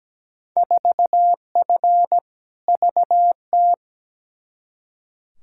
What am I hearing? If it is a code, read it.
Morse code, heard as a keyed tone, about 17 wpm, 700 Hz. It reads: 4F VT
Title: Field Day Exchanges Encore - 17wpm